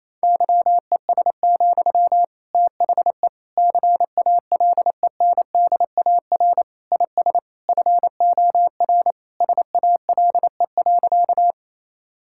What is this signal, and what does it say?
Morse code, heard as a keyed tone, about 28 wpm, 700 Hz. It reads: YES, THE CALENDAR IS FOR SALE.
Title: Sentences from Top 1600 Words 28wpm